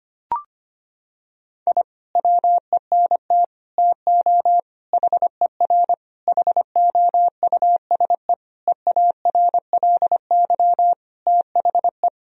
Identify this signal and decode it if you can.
Morse code, heard as a keyed tone, about 25 wpm, 700 Hz. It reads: I WENT TO HER HOUSE EARLY THE
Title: Sentences from Top 1000 Words - Review Part 2 - 25wpm